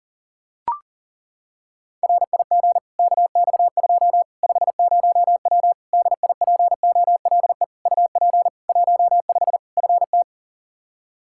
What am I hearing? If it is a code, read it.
Morse code, heard as a keyed tone, about 37 wpm, 700 Hz. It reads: RIG KX2 50W DIPOLE UP 15 FT
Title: K1USN SST QSO Practice 40wpm